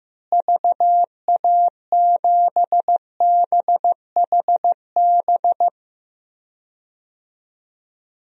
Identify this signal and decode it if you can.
Morse code, heard as a keyed tone, about 15 wpm, 700 Hz. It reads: VA7BHB
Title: DXCC-100 Call Signs - 6 Characters - Repeated 3x - 15wpm